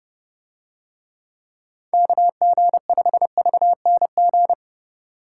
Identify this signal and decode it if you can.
Morse code, heard as a keyed tone, about 30 wpm, 700 Hz. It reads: KG5VNG